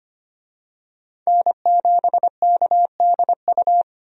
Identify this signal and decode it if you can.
Morse code, heard as a keyed tone, about 25 wpm, 700 Hz. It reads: N7KDU
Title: USA Call Signs - 5 Characters - 25wpm